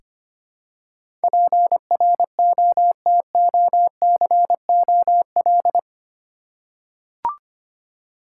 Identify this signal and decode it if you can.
Morse code, heard as a keyed tone, about 25 wpm, 700 Hz. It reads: PROTOCOL
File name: Eight Letter Words - 25wpm